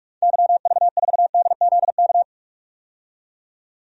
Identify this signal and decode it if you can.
Morse code, heard as a keyed tone, about 44 wpm, 690 Hz. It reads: YV4DZK
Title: DXCC-100 Call Signs - 6 Characters - 45wpm